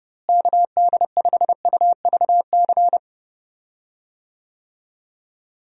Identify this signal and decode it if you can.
Morse code, heard as a keyed tone, about 32 wpm, 700 Hz. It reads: KD5UVC
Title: USA Call Signs - 6 Characters - Repeated 3x - 30wpm